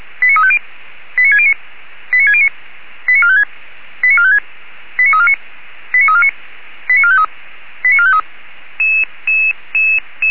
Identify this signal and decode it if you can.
Five-tone selective call (ZVEI), heard as quick runs of five tones, tones of about 70 ms. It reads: 89350 89609 89609 89456 89456 89349 89349 89453 89453